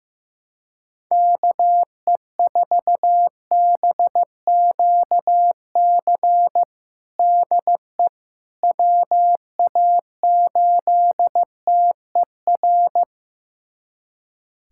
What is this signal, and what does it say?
Morse code, heard as a keyed tone, about 15 wpm, 700 Hz. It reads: KE4BQC DE WA8TER
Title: USA Callsign DE Callsign 15wpm